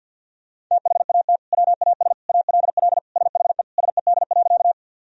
Morse code, heard as a keyed tone, about 51 wpm, 700 Hz. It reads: THAT WAS ALL SHE SED.
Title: Sentences from Top 100 Words and Abbreviations 50wpm